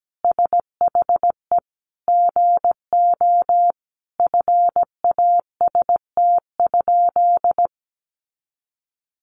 Morse code, heard as a keyed tone, about 17 wpm, 700 Hz. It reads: SHE GO FAST?
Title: Sentences from Top 400 Words 17wpm